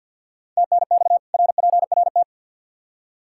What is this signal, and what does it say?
Morse code, heard as a keyed tone, about 53 wpm, 700 Hz. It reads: TNX RPRT
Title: Practice QSO - Exchange 2 - Info - 50wpm